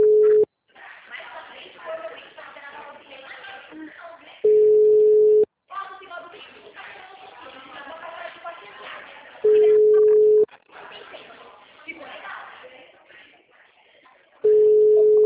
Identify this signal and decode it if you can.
Ringback tone (Europe, 425 Hz), with speech.
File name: out- -2011- - - .32